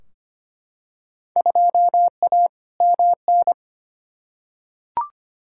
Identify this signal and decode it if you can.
Morse code, heard as a keyed tone, about 25 wpm, 700 Hz. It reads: 2A MN
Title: Field Day Exchanges - 25wpm